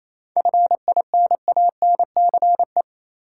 Morse code, heard as a keyed tone, about 28 wpm, 700 Hz. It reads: FINANCE
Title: Seven Letter Words - 28wpm